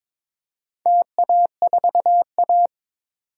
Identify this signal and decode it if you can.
Morse code, heard as a keyed tone, about 22 wpm, 700 Hz. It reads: TA4A
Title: DXCC-100 Call Signs - 4 Characters - 22wpm